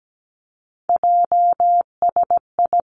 Morse code, heard as a keyed tone, about 17 wpm, 700 Hz. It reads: JSI
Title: Three Challenging Letter-Number Combinations 17wpm